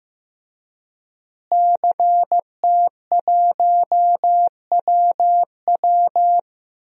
Morse code, heard as a keyed tone, about 15 wpm, 700 Hz. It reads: CT1WW
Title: DXCC-100 Call Signs - 5 Characters - Repeated 3x - 15wpm